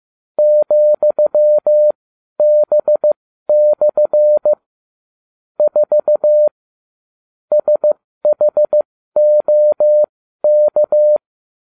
Morse code, heard as a keyed tone, about 15 wpm, spaced as for 12 wpm, 600 Hz. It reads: ,B/ 4 SHOK